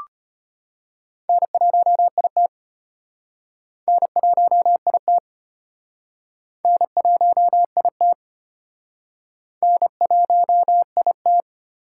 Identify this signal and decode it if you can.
Morse code, heard as a keyed tone, about 32 wpm, 700 Hz. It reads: N1IT N1IT N1IT N1IT